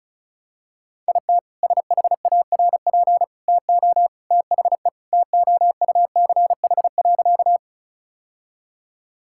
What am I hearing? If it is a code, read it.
Morse code, heard as a keyed tone, about 36 wpm, 700 Hz. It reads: IT SHARP TO THE TOUCH.
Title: Sentences from Top 1600 Words 35wpm